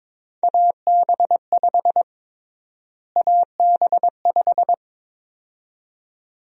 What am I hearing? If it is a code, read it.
Morse code, heard as a keyed tone, about 22 wpm, 700 Hz. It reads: AB5 AB5
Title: USA Call Sign Prefixes - Repeated 2x - 22wpm